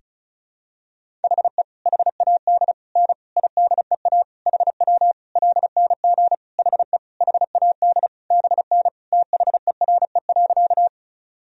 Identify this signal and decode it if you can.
Morse code, heard as a keyed tone, about 34 wpm, 700 Hz. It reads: HE HAD N IDEA HW LNG HE HAD BN THERE.
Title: Sentences from Top 300 Words and Abbreviations 35wpm